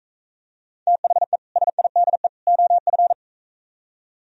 Morse code, heard as a keyed tone, about 43 wpm, 700 Hz. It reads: THE SIDE OF